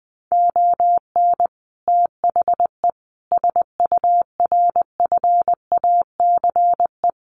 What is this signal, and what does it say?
Morse code, heard as a keyed tone, about 20 wpm, 700 Hz. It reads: ON THE SURFACE
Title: Sets of 5 Words Encore 20wpm